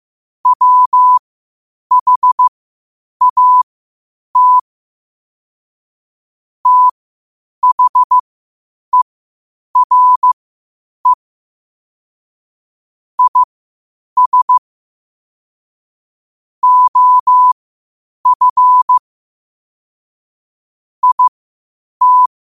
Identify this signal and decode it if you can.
Morse code, heard as a keyed tone, about 15 wpm, spaced as for 8 wpm, 1000 Hz. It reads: WHAT THERE IS OF IT
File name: qotdfiles / 8wpm